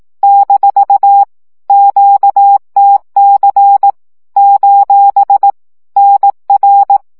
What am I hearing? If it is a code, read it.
Morse code, heard as a keyed tone, about 18 wpm, 800 Hz. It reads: - QTC : NR
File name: morze vizsga